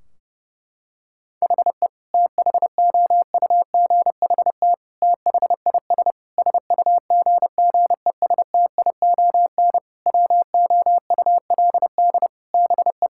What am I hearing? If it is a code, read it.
Morse code, heard as a keyed tone, about 32 wpm, 700 Hz. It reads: HE THOUGHT THIS SUGGESTION WOULD BE
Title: Sentences from Top 1600 Words 30wpm